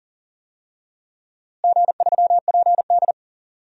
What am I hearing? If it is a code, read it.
Morse code, heard as a keyed tone, about 40 wpm, 700 Hz. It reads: G3PD
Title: DXCC-100 Call Signs - 4 Characters - Repeated 2x - 40wpm